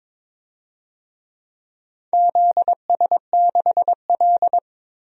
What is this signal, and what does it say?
Morse code, heard as a keyed tone, about 22 wpm, 700 Hz. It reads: ZS6L
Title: DXCC-100 Call Signs - 4 Characters - 22wpm